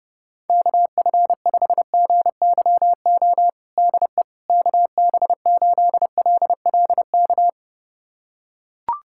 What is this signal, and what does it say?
Morse code, heard as a keyed tone, about 32 wpm, 700 Hz. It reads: KF5GYO DE KB8LLK
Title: USA Callsign DE Callsign 30wpm